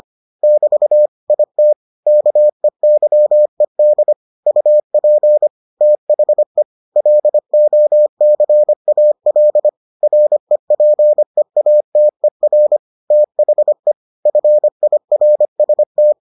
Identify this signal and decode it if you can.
Morse code, heard as a keyed tone, about 25 wpm, 600 Hz. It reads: <BT> IT KEYED UP THE LOCAL REPEATER THE FIRST